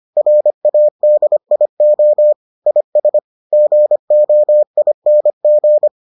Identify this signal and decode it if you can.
Morse code, heard as a keyed tone, about 25 wpm, 600 Hz. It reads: RADIO IS GOING